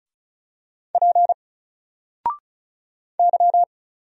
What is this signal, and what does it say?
Morse code, heard as a keyed tone, about 36 wpm, 700 Hz. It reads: P Y
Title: Single Letter-Number 35wpm